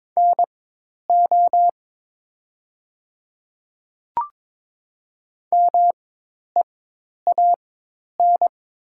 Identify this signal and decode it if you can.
Morse code, heard as a keyed tone, about 22 wpm, spaced as for 10 wpm, 700 Hz. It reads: NO MEAN
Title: 4x Character Spacing - Top 300 Words - 22wpm